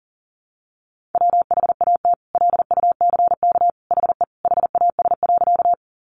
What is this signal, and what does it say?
Morse code, heard as a keyed tone, about 40 wpm, 700 Hz. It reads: WHAT LUCK HE HAS.
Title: Sentences from Top 1500 Words 40wpm